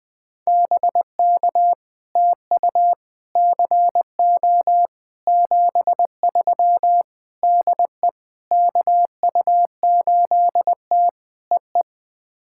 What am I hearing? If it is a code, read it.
Morse code, heard as a keyed tone, about 20 wpm, 700 Hz. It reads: BK TU CO 73 DE KU8T EE
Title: POTA QSO Practice - 20wpm